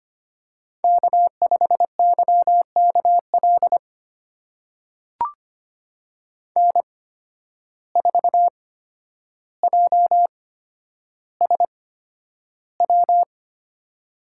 Morse code, heard as a keyed tone, about 25 wpm, spaced as for 11 wpm, 700 Hz. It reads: K5YKL N4JSW
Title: USA Call Signs - 5 Characters - 8x Character Spacing - 25wpm